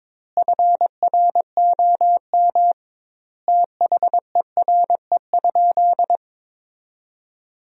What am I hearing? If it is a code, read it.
Morse code, heard as a keyed tone, about 22 wpm, 700 Hz. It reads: FROM THERE?
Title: 2x Word-Spacing - Sentences from Top 100 Words - 22wpm